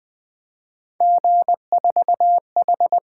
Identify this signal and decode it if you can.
Morse code, heard as a keyed tone, about 20 wpm, 700 Hz. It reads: G4H
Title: Three Letter-Number Combinations - 20wpm